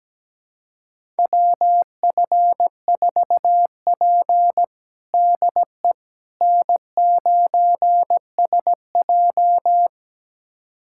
Morse code, heard as a keyed tone, about 17 wpm, 700 Hz. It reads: WF4P DE N9SJ
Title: USA Callsign DE Callsign - 4 Characters - 17wpm